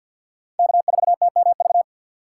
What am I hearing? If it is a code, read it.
Morse code, heard as a keyed tone, about 49 wpm, 700 Hz. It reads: K4TMV
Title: USA Call Signs - 5 Characters - 4x Character Spacing - 50wpm